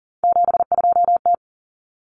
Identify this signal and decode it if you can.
Morse code, heard as a keyed tone, about 40 wpm, 700 Hz. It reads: 72T
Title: Common Cut Numbers - 3-Digit 40wpm